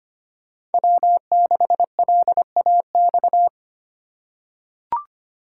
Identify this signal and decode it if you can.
Morse code, heard as a keyed tone, about 25 wpm, 700 Hz. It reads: W6LAX
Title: K1USN SST QSO Practice - 25wpm